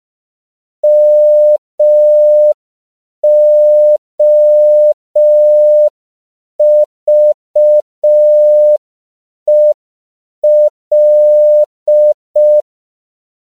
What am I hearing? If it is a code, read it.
Morse code, heard as a keyed tone, about 5 wpm, 600 Hz. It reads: MOVEL
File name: CW / 5PPM